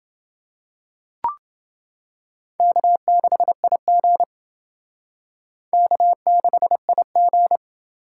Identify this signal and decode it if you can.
Morse code, heard as a keyed tone, about 29 wpm, 700 Hz. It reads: K6IG K6IG